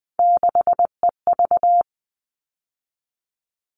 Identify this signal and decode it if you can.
Morse code, heard as a keyed tone, about 20 wpm, 700 Hz. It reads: 6EV
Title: All Cut Numbers - 3-Digit 20wpm